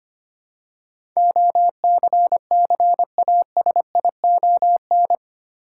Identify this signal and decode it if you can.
Morse code, heard as a keyed tone, about 25 wpm, 700 Hz. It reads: OCCASION
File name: Eight Letter Words - 25wpm